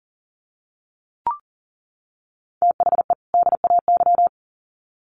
Morse code, heard as a keyed tone, about 40 wpm, 700 Hz. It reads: THE DAY
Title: Sets of 2 Words 40wpm